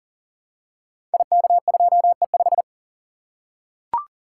Morse code, heard as a keyed tone, about 41 wpm, 700 Hz. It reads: IK2E5
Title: Code Groups - 5 Letter:Numbers - 40wpm